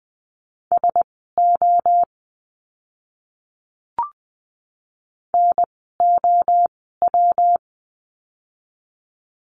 Morse code, heard as a keyed tone, about 20 wpm, spaced as for 14 wpm, 700 Hz. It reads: SO NOW
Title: 230 Lesson 37f - Letter x - Words 20wpm